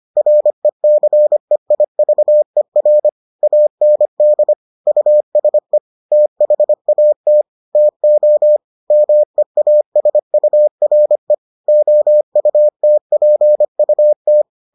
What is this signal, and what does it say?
Morse code, heard as a keyed tone, about 25 wpm, 600 Hz. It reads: RECEIVER AND USE THAT TO MEASURE OUTPUT